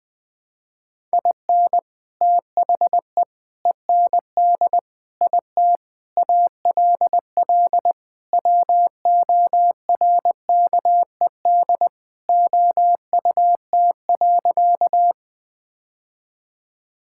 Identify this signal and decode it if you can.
Morse code, heard as a keyed tone, about 21 wpm, 700 Hz. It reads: IN THE END IT ALL WORKED OUT.
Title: Sentences with 3-Word Phrases - 20wpm